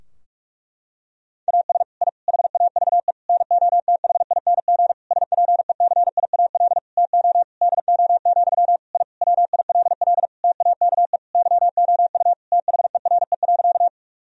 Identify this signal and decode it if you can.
Morse code, heard as a keyed tone, about 46 wpm, 700 Hz. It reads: AS I HAVE NOTHING SPECIAL TO DO, I WILL TAKE YOU THERE.
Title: Sentences from Top 1000 Words - Review Part 3 - 45wpm